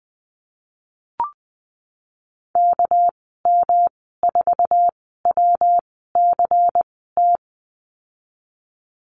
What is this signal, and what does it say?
Morse code, heard as a keyed tone, about 20 wpm, 700 Hz. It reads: K M 4 W C T
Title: 231 Lesson 37g - Letter x - Call Signs 20wpm